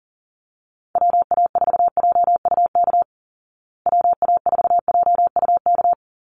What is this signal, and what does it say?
Morse code, heard as a keyed tone, about 40 wpm, 700 Hz. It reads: WA4JUK WA4JUK
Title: Call Signs Repeated 3x - 40wpm